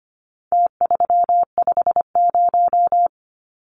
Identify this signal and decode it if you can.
Morse code, heard as a keyed tone, about 25 wpm, 700 Hz. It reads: T350
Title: Numbers - Contest Style - 4-Digit 25wpm